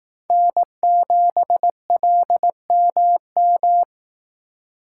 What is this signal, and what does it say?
Morse code, heard as a keyed tone, about 18 wpm, 700 Hz. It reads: N7LMM
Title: Speed Racing - USA 1x3 Call Signs 15wpm